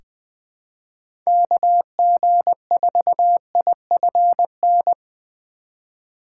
Morse code, heard as a keyed tone, about 20 wpm, 700 Hz. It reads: KG4IFN
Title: USA Call Signs - 6 Characters - 4x Character Spacing - 20wpm